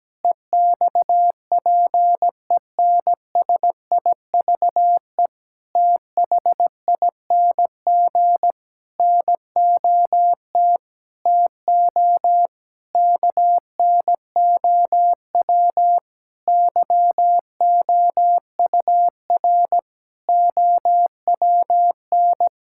Morse code, heard as a keyed tone, about 17 wpm, 700 Hz. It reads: EXPENSIVE THING NOT TO KNOW YOUR OWN
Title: Sentences from Top 1000 Words - Review Part 6 - 17wpm